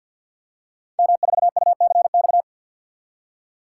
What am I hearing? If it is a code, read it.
Morse code, heard as a keyed tone, about 52 wpm, 700 Hz. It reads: M4UCX
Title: Code Groups - 5 Letter:Numbers - 50wpm